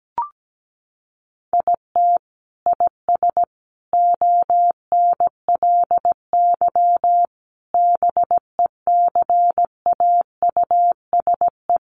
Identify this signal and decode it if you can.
Morse code, heard as a keyed tone, about 17 wpm, 700 Hz. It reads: IT IS ONLY BECAUSE
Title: Sentences from Top 1400 Words 17wpm